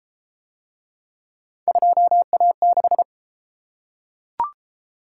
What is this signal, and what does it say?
Morse code, heard as a keyed tone, about 31 wpm, 700 Hz. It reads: 2A6